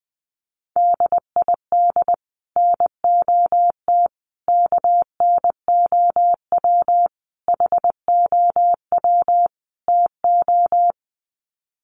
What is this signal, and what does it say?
Morse code, heard as a keyed tone, about 20 wpm, 700 Hz. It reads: DID NOT KNOW HOW TO
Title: Sets of 5 Words 20wpm